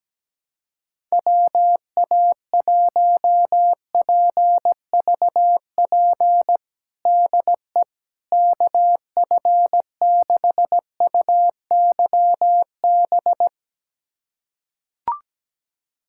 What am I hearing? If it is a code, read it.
Morse code, heard as a keyed tone, about 17 wpm, 700 Hz. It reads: WA1PVP DE KF6UYB
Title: USA Callsign DE Callsign - 6 Characters - 17wpm